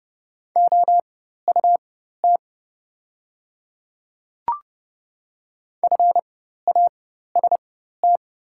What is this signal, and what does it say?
Morse code, heard as a keyed tone, about 30 wpm, spaced as for 14 wpm, 690 Hz. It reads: OUT FAST
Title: 4x Character Spacing - Top 500 Words - 30wpm